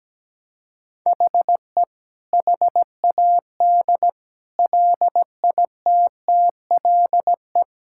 Morse code, heard as a keyed tone, about 17 wpm, 700 Hz. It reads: HE HAD LITTLE
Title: Sentences from Top 1600 Words 17wpm